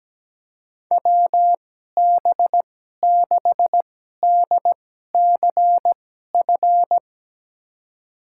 Morse code, heard as a keyed tone, about 17 wpm, spaced as for 12 wpm, 700 Hz. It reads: WB6DCF
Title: USA Call Signs - 2x Character Spacing - 17wpm